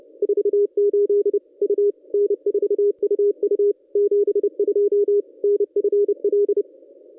Morse code, heard as a keyed tone, about 29 wpm, 410 Hz. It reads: 48 U N4UU 72 NFL
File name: contests / ss-cw-k5tr-op-n5rz -1100